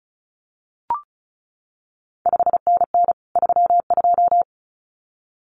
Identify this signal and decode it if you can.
Morse code, heard as a keyed tone, about 36 wpm, 700 Hz. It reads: 5NN 32
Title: CQ WW DX Contest Exchange - 35wpm